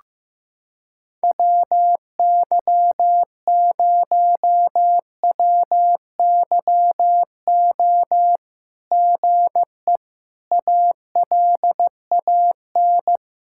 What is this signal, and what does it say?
Morse code, heard as a keyed tone, about 15 wpm, 700 Hz. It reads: WY0WYO GE ALAN
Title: SOTA QSO Practice 15wpm